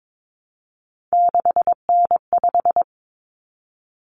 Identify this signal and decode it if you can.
Morse code, heard as a keyed tone, about 22 wpm, 700 Hz. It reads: 6N5
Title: Common Cut Numbers - 3-Digit 22wpm